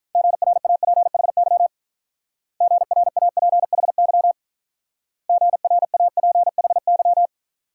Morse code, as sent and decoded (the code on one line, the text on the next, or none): --. .-. .- .--. .... -.-- / --. .-. .- .--. .... -.-- / --. .-. .- .--. .... -.--
GRAPHY GRAPHY GRAPHY